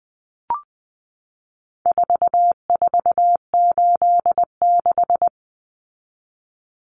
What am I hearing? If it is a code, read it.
Morse code, heard as a keyed tone, about 20 wpm, 700 Hz. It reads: 4486
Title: Numbers - 4-Digit 20wpm